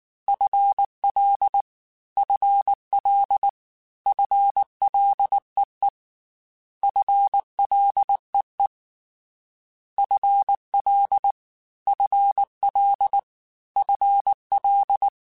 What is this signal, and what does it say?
Morse code, heard as a keyed tone, about 19 wpm, 800 Hz. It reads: FL FL FLEE FLEE FL FL FL
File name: FL / FL mp3s / FL Prefixes 3x2 Twice @18